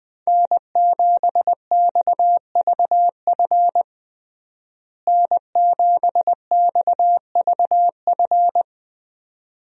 Morse code, heard as a keyed tone, about 20 wpm, 700 Hz. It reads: N7XVF N7XVF
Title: USA Call Signs - 5 Characters - Repeated 2x - 20wpm